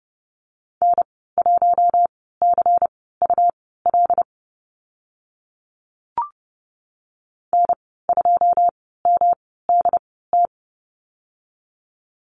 Morse code, heard as a keyed tone, about 30 wpm, spaced as for 17 wpm, 700 Hz. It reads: N1CUL N2MDT
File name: Lesson 19g - Letter w - Call Signs 30wpm